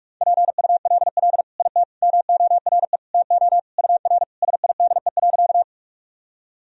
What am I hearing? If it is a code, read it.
Morse code, heard as a keyed tone, about 46 wpm, 700 Hz. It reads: PULL IT MORE TO UR SIDE.
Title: Sentences from Top 400 Words and Abbreviations 45wpm